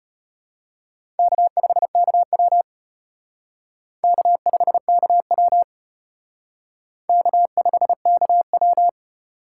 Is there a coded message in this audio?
Yes: Morse code from 1.2 to 8.9 s, heard as a keyed tone, about 32 wpm, 700 Hz. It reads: K5KW K5KW K5KW